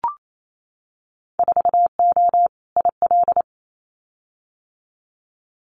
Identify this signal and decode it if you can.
Morse code, heard as a keyed tone, about 28 wpm, 700 Hz. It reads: 4O IL
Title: Winter Field Day Exchanges 28wpm